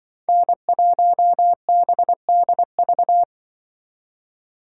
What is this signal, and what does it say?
Morse code, heard as a keyed tone, about 24 wpm, 700 Hz. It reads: N1BDV
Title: Speed Racing - USA 1x3 Call Signs 20wpm